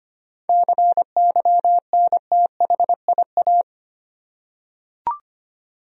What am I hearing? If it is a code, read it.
Morse code, heard as a keyed tone, about 25 wpm, 700 Hz. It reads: CYNTHIA
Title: USA Common Names - 25wpm